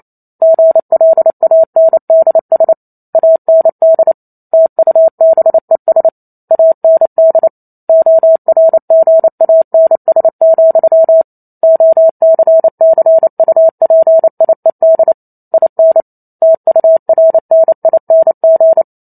Morse code, heard as a keyed tone, about 28 wpm, 650 Hz. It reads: GLANDS AND TUBES AND ORGANS, OCCUPIED IN TURNING